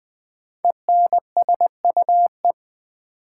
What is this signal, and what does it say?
Morse code, heard as a keyed tone, about 20 wpm, 700 Hz. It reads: ENSUE
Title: 5 Letter Words - First 3 Letters Given - 20wpm